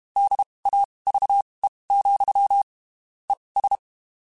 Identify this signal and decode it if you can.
Morse code, heard as a keyed tone, about 32 wpm, spaced as for 23 wpm, 800 Hz. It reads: DAVE, ES
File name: (spaced) / QSOs Spaced-mp3s / WD0BGP 1 @30 sp